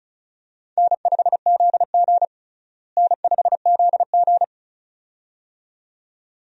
Morse code, heard as a keyed tone, about 34 wpm, 700 Hz. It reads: N5ZG N5ZG
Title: USA Call Signs - 4 Characters - Repeated 2x - 35wpm